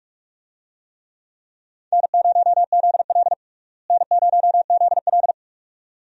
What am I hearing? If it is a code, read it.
Morse code, heard as a keyed tone, about 46 wpm, 700 Hz. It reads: N0ZL N0ZL
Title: USA Call Signs - 4 Characters - Repeated 2x - 45wpm